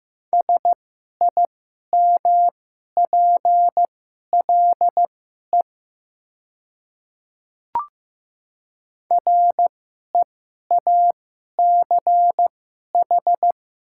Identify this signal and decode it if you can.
Morse code, heard as a keyed tone, about 16 wpm, spaced as for 11 wpm, 700 Hz. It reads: SIMPLE REACH
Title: 2x Character Spacing - Top 500 Words - 15wpm